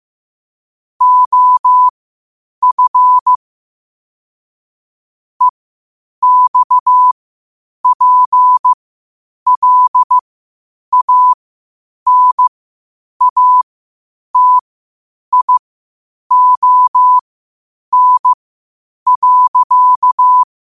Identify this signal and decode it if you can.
Morse code, heard as a keyed tone, about 15 wpm, spaced as for 8 wpm, 1000 Hz. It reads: OF EXPLANATION.
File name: qotdfiles / 8wpm